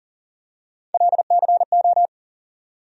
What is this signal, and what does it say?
Morse code, heard as a keyed tone, about 40 wpm, 700 Hz. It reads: LCO
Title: USA Call Sign Postfixes - 3 Letters 40wpm